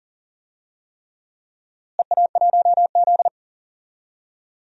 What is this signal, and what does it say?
Morse code, heard as a keyed tone, about 40 wpm, 700 Hz. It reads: EA1Z
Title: DXCC-100 Call Signs - 4 Characters - Repeated 3x - 40wpm